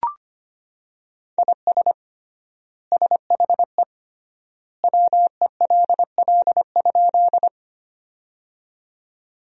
Morse code, heard as a keyed tone, about 25 wpm, 700 Hz. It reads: IS SHE WELL?
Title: 3x Word-Spacing - Sentences from Top 200 Words - 25wpm